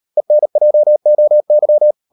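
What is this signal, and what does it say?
Morse code, heard as a keyed tone, about 38 wpm, 600 Hz. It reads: ENJOY
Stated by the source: Morse code practice at 38 WPM, sending Ken Follett's novel Never in English, from CW med Gnister.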